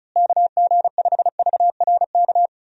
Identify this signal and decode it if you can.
Morse code, heard as a keyed tone, about 34 wpm, 700 Hz. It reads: KG5VRK
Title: USA Call Signs - 6 Characters - Repeated 2x - 35wpm